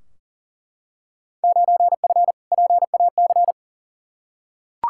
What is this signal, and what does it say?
Morse code, heard as a keyed tone, about 40 wpm, 700 Hz. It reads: 9F PAC
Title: Field Day Exchanges Encore - 40wpm